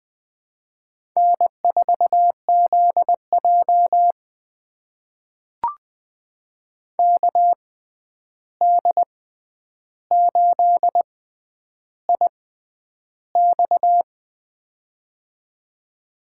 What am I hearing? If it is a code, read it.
Morse code, heard as a keyed tone, about 20 wpm, spaced as for 10 wpm, 700 Hz. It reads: N4ZJ KD8IX
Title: USA Call Signs - 6x Character Spacing - 20wpm